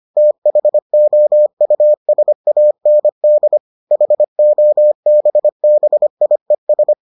Morse code, heard as a keyed tone, about 25 wpm, 600 Hz. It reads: THOUSAND HOBBIES